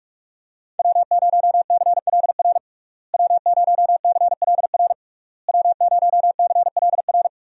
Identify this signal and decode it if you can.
Morse code, heard as a keyed tone, about 46 wpm, 700 Hz. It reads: W0CLR W0CLR W0CLR
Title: USA Call Signs - 5 Characters - Repeated 3x - 45wpm